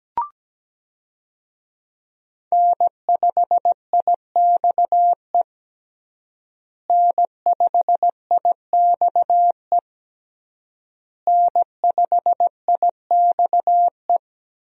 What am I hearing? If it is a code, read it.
Morse code, heard as a keyed tone, about 17 wpm, 700 Hz. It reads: N5IXE N5IXE N5IXE
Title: USA Call Signs - 5 Characters - Repeated 3x - 17wpm